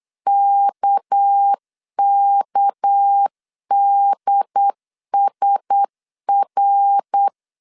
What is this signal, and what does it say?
Morse code, heard as a keyed tone, about 8 wpm, 800 Hz. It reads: KKDSR
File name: LEZIONI / 6 PKX0 / 6 PK 8wpm 3003